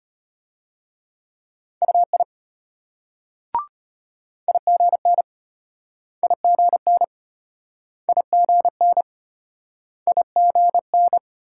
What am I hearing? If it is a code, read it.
Morse code, heard as a keyed tone, about 35 wpm, 700 Hz. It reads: UI IGN IGN IGN IGN